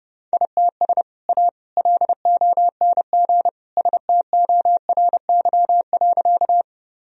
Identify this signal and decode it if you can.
Morse code, heard as a keyed tone, about 30 wpm, 700 Hz. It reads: ITS A LONG STORY.
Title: Sentences from Top 1000 Words - Review Part 1 - 30wpm